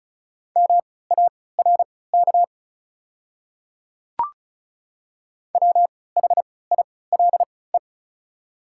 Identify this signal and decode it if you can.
Morse code, heard as a keyed tone, about 36 wpm, spaced as for 20 wpm, 700 Hz. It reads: MARK WHILE
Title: 3x Character Spacing - Top 300 Words - 35wpm